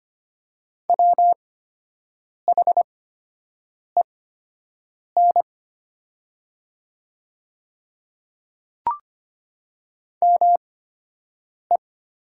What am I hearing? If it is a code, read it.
Morse code, heard as a keyed tone, about 25 wpm, spaced as for 7 wpm, 700 Hz. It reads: WHEN ME
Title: 8x Character Spacing - Top 200 Words - 25wpm